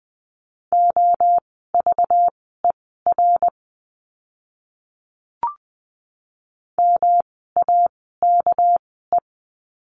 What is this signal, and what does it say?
Morse code, heard as a keyed tone, about 20 wpm, spaced as for 15 wpm, 700 Hz. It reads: OVER MAKE
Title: 230 Lesson 37f - Letter x - Words 20wpm